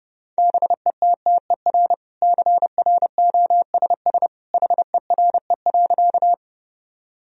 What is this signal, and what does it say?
Morse code, heard as a keyed tone, about 32 wpm, 700 Hz. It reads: BETTER CROSS HERE.
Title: Sentences from Top 1000 Words - Review Part 1 - 30wpm